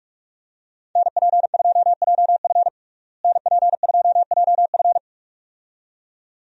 Morse code, heard as a keyed tone, about 41 wpm, 700 Hz. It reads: NP2JF NP2JF
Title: USA Call Signs - 5 Characters - Repeated 2x - 45wpm